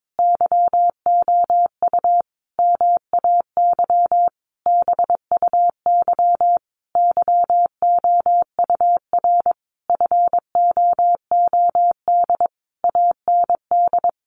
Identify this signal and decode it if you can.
Morse code, heard as a keyed tone, about 22 wpm, 700 Hz. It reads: YOU MAY BUY YOUR FOOD AND
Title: Sentences from Top 800 Words 22wpm